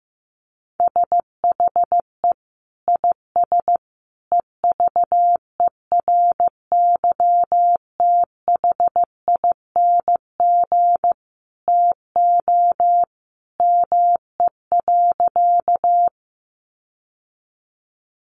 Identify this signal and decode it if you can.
Morse code, heard as a keyed tone, about 15 wpm, 700 Hz. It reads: SHE IS EVERYTHING TO ME.
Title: Sentences from Top 700 Words 15wpm